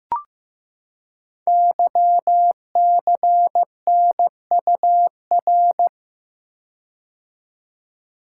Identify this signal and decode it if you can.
Morse code, heard as a keyed tone, about 15 wpm, 700 Hz. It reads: YCNUR
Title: Code Groups - 5 Letter:Numbers - 15wpm